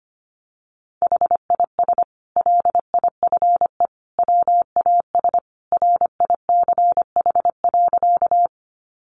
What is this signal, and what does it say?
Morse code, heard as a keyed tone, about 25 wpm, 700 Hz. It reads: HIS LIFE WAS RICH.
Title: Sentences from Top 1100 Words 25wpm